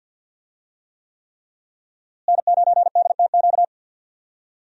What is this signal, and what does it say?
Morse code, heard as a keyed tone, about 47 wpm, 700 Hz. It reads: N9DTQ
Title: USA Call Signs - 5 Characters - Repeated 3x - 50wpm